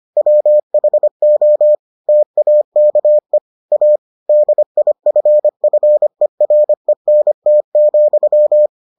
Morse code, heard as a keyed tone, about 25 wpm, 600 Hz. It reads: WHO TAKE A DIFFERENT,